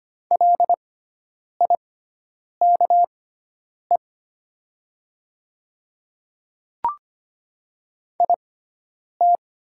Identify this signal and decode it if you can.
Morse code, heard as a keyed tone, about 25 wpm, spaced as for 9 wpm, 700 Hz. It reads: LIKE IT
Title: 6x Character Spacing - Top 100 Words - 25wpm